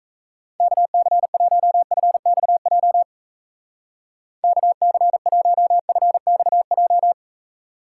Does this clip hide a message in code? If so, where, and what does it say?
Yes: Morse code from 0.6 to 7.1 s, heard as a keyed tone, about 36 wpm, 700 Hz. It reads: KC1FXJ KC1FXJ